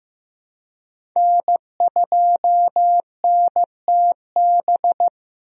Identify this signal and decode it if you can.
Morse code, heard as a keyed tone, about 15 wpm, 700 Hz. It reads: N2NTB
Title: USA Call Signs - 5 Characters - Repeated 3x - 15wpm